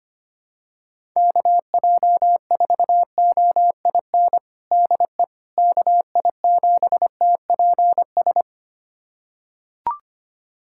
Morse code, heard as a keyed tone, about 25 wpm, 700 Hz. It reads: KJ4OIN DE KI7TPS
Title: USA Callsign DE Callsign 25wpm